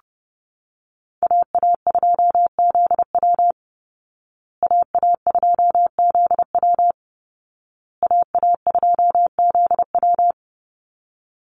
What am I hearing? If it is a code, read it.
Morse code, heard as a keyed tone, about 30 wpm, 700 Hz. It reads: AA2ZW AA2ZW AA2ZW
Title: Call Signs Repeated 2x - 30wpm